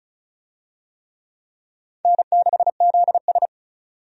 Morse code, heard as a keyed tone, about 36 wpm, 700 Hz. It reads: N6ZS
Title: USA Call Signs - 4 Characters - 35wpm